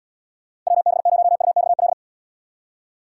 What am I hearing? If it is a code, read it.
Morse code, heard as a keyed tone, about 79 wpm, 700 Hz. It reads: WB0UXL